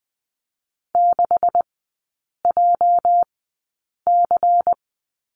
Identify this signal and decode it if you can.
Morse code, heard as a keyed tone, about 19 wpm, spaced as for 8 wpm, 700 Hz. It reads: 6JC
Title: 205 Lesson 33e - Letter j - Three Characters 20wpm